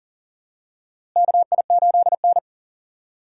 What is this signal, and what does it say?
Morse code, heard as a keyed tone, about 40 wpm, 700 Hz. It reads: KI8N
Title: USA Call Signs - 4 Characters - Repeated 2x - 40wpm